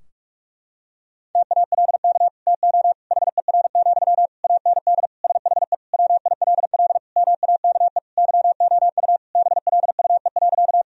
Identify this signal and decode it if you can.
Morse code, heard as a keyed tone, about 43 wpm, 700 Hz. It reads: TALK TO HER, AND SHE WILL MAKE YOU BLUE.
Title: Sentences from Top 1000 Words - Review Part 3 - 45wpm